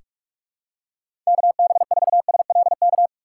Morse code, heard as a keyed tone, about 46 wpm, 700 Hz. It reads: KB4SLK
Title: USA Call Signs - 2x Character Spacing - 45wpm